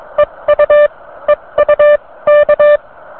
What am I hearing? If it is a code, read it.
Morse code, heard as a keyed tone, about 23 wpm, spaced as for 19 wpm, 590 Hz. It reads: EU EU K